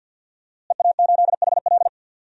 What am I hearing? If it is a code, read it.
Morse code, heard as a keyed tone, about 52 wpm, 700 Hz. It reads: EA8HL
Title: DXCC-100 Call Signs - 5 Characters - Repeated 2x - 50wpm